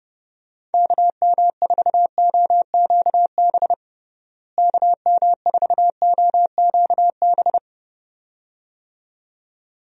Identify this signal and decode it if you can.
Morse code, heard as a keyed tone, about 30 wpm, 700 Hz. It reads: KM4OQB KM4OQB
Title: USA Call Signs - 6 Characters - Repeated 2x - 30wpm